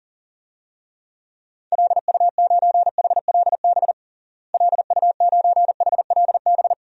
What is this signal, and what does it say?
Morse code, heard as a keyed tone, about 40 wpm, 700 Hz. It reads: LU9HLB LU9HLB
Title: DXCC-100 Call Signs - 6 Characters - Repeated 2x - 40wpm